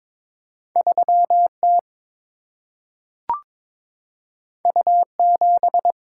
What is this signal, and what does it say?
Morse code, heard as a keyed tone, about 22 wpm, 700 Hz. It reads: 3 T U 7
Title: Single-Letter Single-Number 22wpm